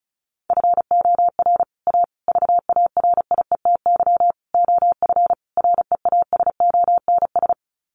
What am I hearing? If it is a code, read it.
Morse code, heard as a keyed tone, about 34 wpm, 700 Hz. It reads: FOR A VARIETY OF REASONS
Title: Sets of 5 Words Encore 35wpm